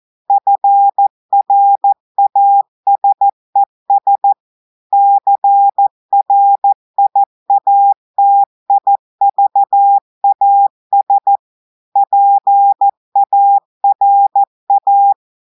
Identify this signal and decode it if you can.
Morse code, heard as a keyed tone, about 14 wpm, 800 Hz. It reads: FRASES CRIATIVAS PARA
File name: audios / portugais / portugais texte en clair 2 vitesse 14 mots minutes